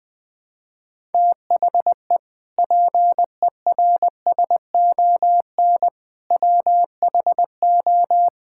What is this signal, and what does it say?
Morse code, heard as a keyed tone, about 20 wpm, 700 Hz. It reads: THE PERSON WHO
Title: Sentences with 8-Word Phrases - 20wpm